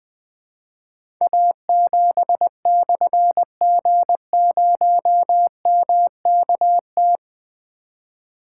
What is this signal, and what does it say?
Morse code, heard as a keyed tone, about 20 wpm, 700 Hz. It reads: A7/G0MKT
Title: Difficult Call Signs 20wpm